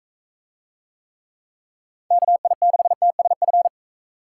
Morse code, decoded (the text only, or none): KI6TSF